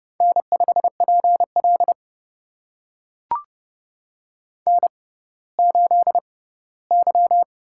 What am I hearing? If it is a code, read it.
Morse code, heard as a keyed tone, about 29 wpm, spaced as for 17 wpm, 700 Hz. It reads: N5PL N8Y
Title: USA Call Signs - 4 Characters - 6x Character Spacing - 30wpm